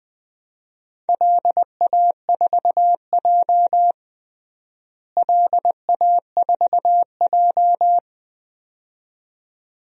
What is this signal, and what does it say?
Morse code, heard as a keyed tone, about 21 wpm, 700 Hz. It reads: LA4J LA4J
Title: DXCC-100 Call Signs - 4 Characters - Repeated 2x - 20wpm